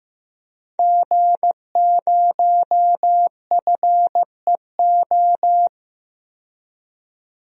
Morse code, heard as a keyed tone, about 15 wpm, 700 Hz. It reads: G0FEO
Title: DXCC-100 Call Signs - 5 Characters - 15wpm